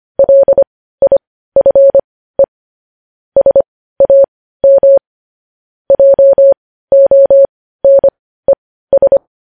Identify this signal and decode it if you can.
Morse code, heard as a keyed tone, about 25 wpm, spaced as for 15 wpm, 550 Hz. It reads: LIFE SAM JONES
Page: news headlines in morse code at 15 wpm